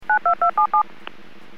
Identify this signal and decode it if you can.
DTMF, heard as telephone touch tones, tones of about 100 ms. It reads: 92277